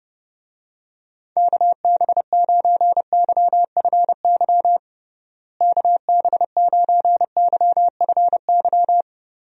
Morse code, heard as a keyed tone, about 30 wpm, 700 Hz. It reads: KB9YFY KB9YFY
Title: USA Call Signs - 6 Characters - Repeated 3x - 30wpm